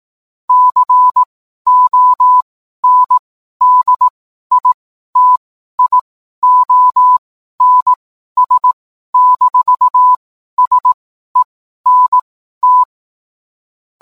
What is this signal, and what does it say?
Morse code, heard as a keyed tone, about 18 wpm, spaced as for 12 wpm, 1000 Hz. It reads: CONDITIONS-SENT
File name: cwblog / cw mp3 directory / mem morse 12